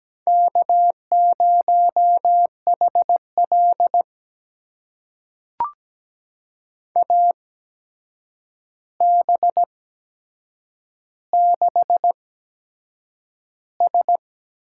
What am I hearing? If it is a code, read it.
Morse code, heard as a keyed tone, about 17 wpm, spaced as for 7 wpm, 700 Hz. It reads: K0HL AB6S
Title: USA Call Signs - 4 Characters - 8x Character Spacing - 17wpm